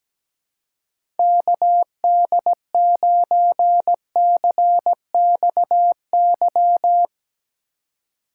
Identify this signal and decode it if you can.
Morse code, heard as a keyed tone, about 17 wpm, 700 Hz. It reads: KD9CXY
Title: USA Call Signs - 6 Characters - 4x Character Spacing - 17wpm